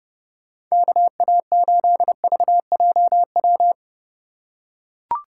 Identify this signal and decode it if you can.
Morse code, heard as a keyed tone, about 30 wpm, 700 Hz. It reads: KA8VJW
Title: USA Call Signs - 3x Character Spacing - 30wpm